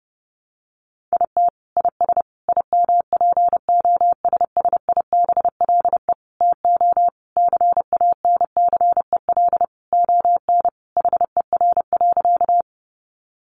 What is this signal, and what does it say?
Morse code, heard as a keyed tone, about 30 wpm, 700 Hz. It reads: IT IS IMPOSSIBLE TO CANCEL ON HER.
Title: Sentences from Top 1200 Words 30wpm